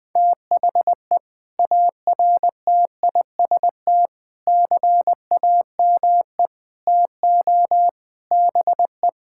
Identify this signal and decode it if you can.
Morse code, heard as a keyed tone, about 19 wpm, 700 Hz. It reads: THE ARTIST CAME TO BE
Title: Sentences with 6-Word Phrases - Spoken Beginning - 20wpm